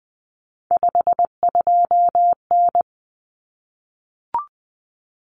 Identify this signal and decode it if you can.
Morse code, heard as a keyed tone, about 20 wpm, 700 Hz. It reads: 52N
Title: Common Cut Numbers - 3-Digit 20wpm